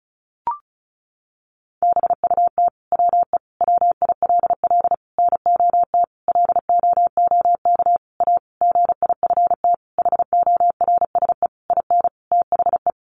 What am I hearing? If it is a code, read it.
Morse code, heard as a keyed tone, about 34 wpm, 700 Hz. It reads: BUT WE WILL NOT LOOK A GIFT HORSE IN THE
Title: Sentences from Top 1500 Words 35wpm